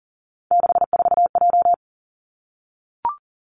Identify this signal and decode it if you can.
Morse code, heard as a keyed tone, about 42 wpm, 700 Hz. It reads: 64J
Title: Three Challenging Letter-Number Combinations 40wpm